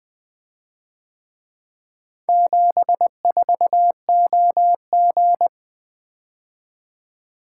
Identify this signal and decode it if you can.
Morse code, heard as a keyed tone, about 23 wpm, 700 Hz. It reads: 74OG
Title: Speed-Racing - Four Letter-Number Combinations - 20wpm